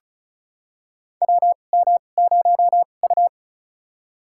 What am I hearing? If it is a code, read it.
Morse code, heard as a keyed tone, about 36 wpm, spaced as for 25 wpm, 700 Hz. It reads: WM0U
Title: USA Call Signs - 2x Character Spacing - 35wpm